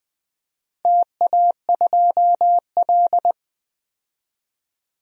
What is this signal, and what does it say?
Morse code, heard as a keyed tone, about 20 wpm, 700 Hz. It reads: TA2L
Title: DXCC-100 Call Signs - 4 Characters - 20wpm